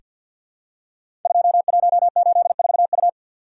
Morse code, heard as a keyed tone, about 48 wpm, 700 Hz. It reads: 2184U
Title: Code Groups - 5 Letter:Numbers - 50wpm